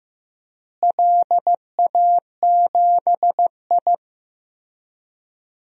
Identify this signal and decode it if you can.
Morse code, heard as a keyed tone, about 15 wpm, 700 Hz. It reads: LA7I
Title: DXCC-100 Call Signs - 4 Characters - Repeated 3x - 15wpm